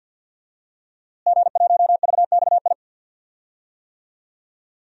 Morse code, heard as a keyed tone, about 47 wpm, 700 Hz. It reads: G1VXI
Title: DXCC-100 Call Signs - 5 Characters - Repeated 3x - 50wpm